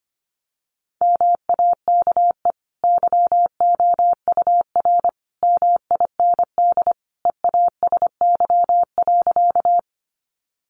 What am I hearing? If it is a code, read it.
Morse code, heard as a keyed tone, about 25 wpm, 700 Hz. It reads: MAKE YOUR MIND EASY.
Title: Sentences from Top 700 Words 25wpm